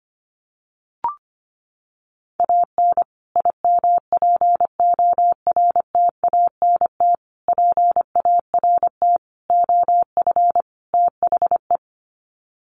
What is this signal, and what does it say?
Morse code, heard as a keyed tone, about 25 wpm, 700 Hz. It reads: AN IMPORTANT PART OF THE
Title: Sets of 5 Words Encore 25wpm